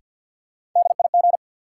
Morse code, heard as a keyed tone, about 53 wpm, 700 Hz. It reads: DIG
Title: Sentences from Top 1600 Words - New Word Drill Alt - 50wpm